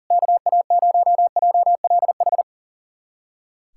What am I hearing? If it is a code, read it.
Morse code, heard as a keyed tone, about 40 wpm, 700 Hz. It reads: KA0JLH
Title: USA Callsign DE Callsign - 6 Characters - 40wpm